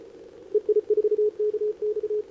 Morse code, heard as a keyed tone, about 33 wpm, 410 Hz. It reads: EI4KX
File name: CQWWCW2025 / NedeleOdpol 20m XW4KV Scan